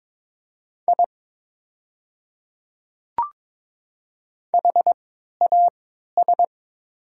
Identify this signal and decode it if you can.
Morse code, heard as a keyed tone, about 22 wpm, spaced as for 13 wpm, 690 Hz. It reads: I HAS
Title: 3x Character Spacing - Top 300 Words - 22wpm